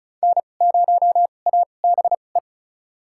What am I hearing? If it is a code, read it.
Morse code, heard as a keyed tone, about 36 wpm, spaced as for 25 wpm, 700 Hz. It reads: N0ABE
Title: USA Call Signs - 5 Characters - 2x Character Spacing - 35wpm